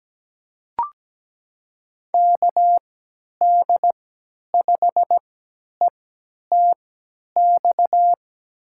Morse code, heard as a keyed tone, about 17 wpm, spaced as for 10 wpm, 700 Hz. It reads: KD5ETX
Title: USA Call Signs - 6 Characters - 3x Character Spacing - 17wpm